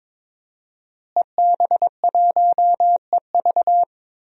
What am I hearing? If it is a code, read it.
Morse code, heard as a keyed tone, about 22 wpm, 700 Hz. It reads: EB1EV
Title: DXCC-100 Call Signs - 5 Characters - Repeated 2x - 22wpm